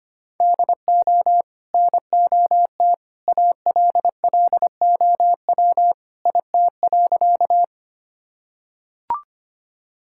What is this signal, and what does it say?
Morse code, heard as a keyed tone, about 25 wpm, 700 Hz. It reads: DO NOT ALLOW IT.
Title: Sentences from Top 1000 Words - Review Part 5 - 25wpm